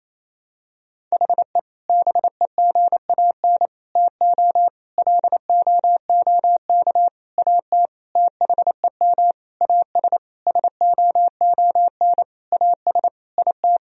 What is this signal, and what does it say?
Morse code, heard as a keyed tone, about 28 wpm, 700 Hz. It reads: HE BEGAN TO LOOK AT THEM AS SOON AS IT
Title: Sentences from Top 1000 Words - Review Part 1 - 28wpm